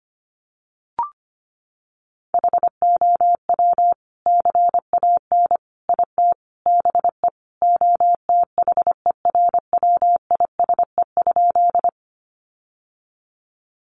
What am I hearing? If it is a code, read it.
Morse code, heard as a keyed tone, about 25 wpm, 700 Hz. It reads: HOW CAN IT BE OTHERWISE?
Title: Sentences from Top 1200 Words 25wpm